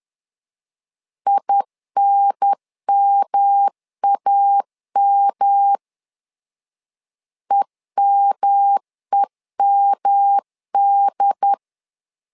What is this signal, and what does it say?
Morse code, heard as a keyed tone, about 11 wpm, 800 Hz. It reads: INMAM EMEMD
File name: LEZIONI / 3 MWRO / 3 M 10wpm 1828